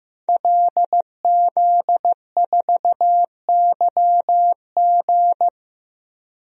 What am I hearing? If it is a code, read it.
Morse code, heard as a keyed tone, about 15 wpm, 700 Hz. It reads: LZ4YG
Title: DXCC-100 Call Signs - 5 Characters - 15wpm